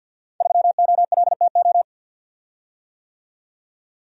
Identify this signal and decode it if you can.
Morse code, heard as a keyed tone, about 47 wpm, 700 Hz. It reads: 3OLTO
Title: Code Groups - 5 Letter:Numbers - 50wpm